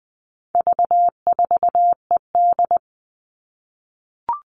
Morse code, heard as a keyed tone, about 20 wpm, 700 Hz. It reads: V4ED
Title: All Cut Numbers - 4-Digit 20wpm